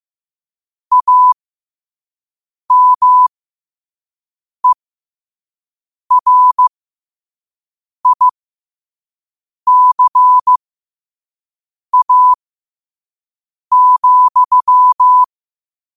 Morse code, heard as a keyed tone, about 15 wpm, spaced as for 5 wpm, 1000 Hz. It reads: AMERICA,
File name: qotdfiles / 5wpm